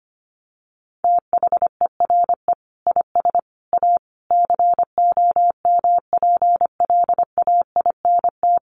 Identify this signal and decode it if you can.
Morse code, heard as a keyed tone, about 25 wpm, 700 Hz. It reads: THERE IS A COMPLAINT
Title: Sentences from Top 1400 Words 25wpm